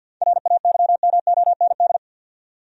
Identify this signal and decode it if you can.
Morse code, heard as a keyed tone, about 50 wpm, 700 Hz. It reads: RAYMOND
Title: USA Common Names - 50wpm